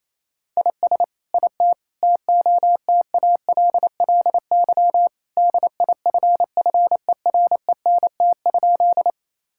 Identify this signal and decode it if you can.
Morse code, heard as a keyed tone, about 28 wpm, 700 Hz. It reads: IS IT TOTALLY DIFFERENT?
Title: Sentences from Top 1600 Words 28wpm